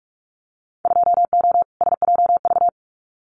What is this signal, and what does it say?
Morse code, heard as a keyed tone, about 46 wpm, 700 Hz. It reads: 2O SJV
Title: Winter Field Day Exchanges Encore 45wpm